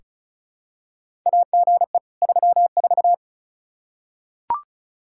Morse code, heard as a keyed tone, about 34 wpm, 700 Hz. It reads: AGE 34
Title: Practice QSO - Exchange 3 - More Info - 35wpm